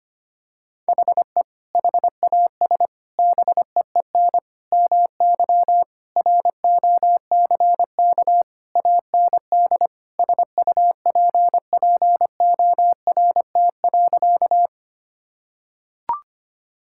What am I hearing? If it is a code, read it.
Morse code, heard as a keyed tone, about 25 wpm, 700 Hz. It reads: HE HAS BEEN MY ROCK AND SUPPORT.
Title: Sentences from Top 1000 Words - Review Part 5 - 25wpm